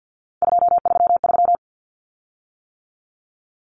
Morse code, heard as a keyed tone, about 48 wpm, 700 Hz. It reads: 233
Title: Common Cut Numbers - 3-Digit 50wpm